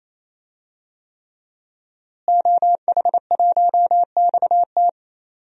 Morse code, heard as a keyed tone, about 27 wpm, 700 Hz. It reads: OH1XT
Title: DXCC-100 Call Signs - 5 Characters - Repeated 3x - 28wpm